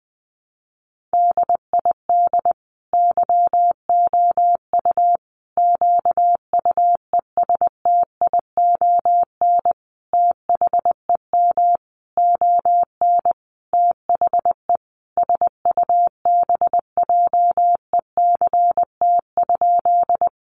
Morse code, heard as a keyed tone, about 20 wpm, 700 Hz. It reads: DID YOU QUESTION THEM ON THE SUBJECT?
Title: Sentences from Top 800 Words 20wpm